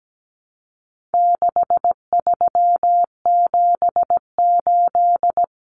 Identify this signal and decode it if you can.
Morse code, heard as a keyed tone, about 17 wpm, 700 Hz. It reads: 6378
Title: Numbers - 4-Digit 17wpm